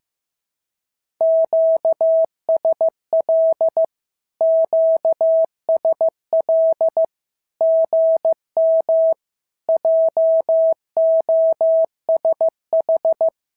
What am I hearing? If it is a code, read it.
Morse code, heard as a keyed tone, about 14 wpm, 650 Hz. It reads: QSL QSL GM JOSH
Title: SOTA QSO Practice 15wpm